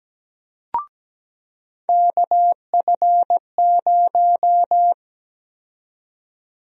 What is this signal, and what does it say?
Morse code, heard as a keyed tone, about 17 wpm, 700 Hz. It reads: KF0
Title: USA Call Sign Prefixes - 17wpm